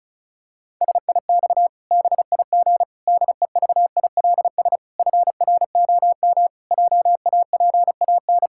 Morse code, heard as a keyed tone, about 34 wpm, 700 Hz. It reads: SIX BIG DEVILS FROM JAPAN
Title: Pangram Challenge v2 - 35wpm